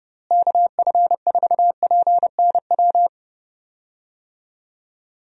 Morse code, heard as a keyed tone, about 30 wpm, 700 Hz. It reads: KF4PNW
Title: USA Call Signs - 6 Characters - Repeated 2x - 30wpm